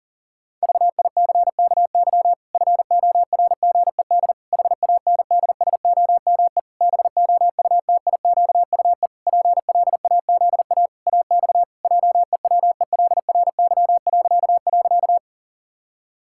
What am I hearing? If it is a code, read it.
Morse code, heard as a keyed tone, about 40 wpm, 700 Hz. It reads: VICKY FORGED HANDSOME BOUTIQUE PLAZA AX JEWELRY..
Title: Pangram Challenge v2 - 40wpm